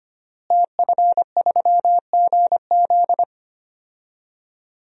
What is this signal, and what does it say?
Morse code, heard as a keyed tone, about 25 wpm, 700 Hz. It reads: TF3GZ
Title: DXCC-100 Call Signs - 5 Characters - Repeated 3x - 25wpm